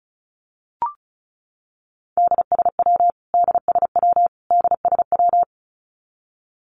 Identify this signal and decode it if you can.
Morse code, heard as a keyed tone, about 34 wpm, 700 Hz. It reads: DSW DSW DSW
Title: Abbreviations Repeated 3x - 35wpm